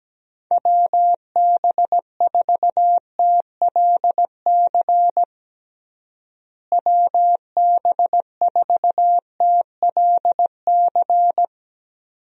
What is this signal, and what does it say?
Morse code, heard as a keyed tone, about 17 wpm, 700 Hz. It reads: WB4TLC WB4TLC
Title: USA Call Signs - 6 Characters - Repeated 2x - 17wpm